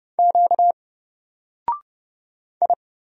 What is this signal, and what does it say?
Morse code, heard as a keyed tone, about 31 wpm, 700 Hz. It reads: Q I
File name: Single-Letter-Number-30wpm